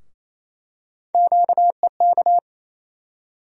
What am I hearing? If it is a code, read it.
Morse code, heard as a keyed tone, about 28 wpm, 700 Hz. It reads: QEK
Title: USA Call Sign Postfixes - 3 Letters 28wpm